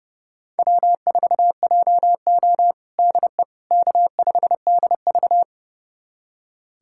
Morse code, heard as a keyed tone, about 30 wpm, 700 Hz. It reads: W4JO DE K5DV
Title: USA Callsign DE Callsign - 4 Characters - 30wpm